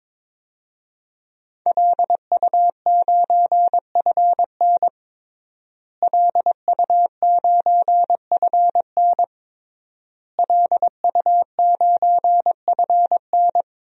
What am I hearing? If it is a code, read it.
Morse code, heard as a keyed tone, about 21 wpm, 700 Hz. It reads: LU9FN LU9FN LU9FN
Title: DXCC-100 Call Signs - 5 Characters - Repeated 3x - 22wpm